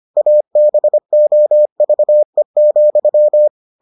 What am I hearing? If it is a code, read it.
Morse code, heard as a keyed tone, about 25 wpm, 600 Hz. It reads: ABOVE,